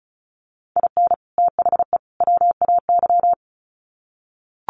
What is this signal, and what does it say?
Morse code, heard as a keyed tone, about 34 wpm, 700 Hz. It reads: IN THE WAY
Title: Sets of 3 Words 35wpm